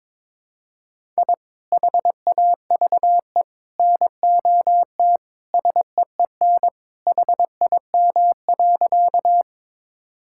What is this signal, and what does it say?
Morse code, heard as a keyed tone, about 22 wpm, 700 Hz. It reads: I HAVE NOT SEEN HIM.
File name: practice / 22wpm / Sentences with 3-Word Phrases - 22wpm